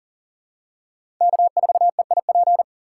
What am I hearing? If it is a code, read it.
Morse code, heard as a keyed tone, about 38 wpm, 700 Hz. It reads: K4EIP
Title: USA Call Signs - 4x Character Spacing - 40wpm